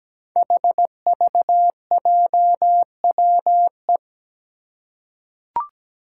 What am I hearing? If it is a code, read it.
Morse code, heard as a keyed tone, about 17 wpm, 700 Hz. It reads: HVJWE
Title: Five Challenging Letter-Number Combinations 17wpm